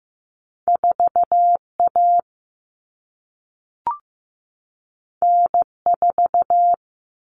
Morse code, heard as a keyed tone, about 15 wpm, 700 Hz. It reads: 4A N4
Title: All Cut Numbers - 2-Digit 15wpm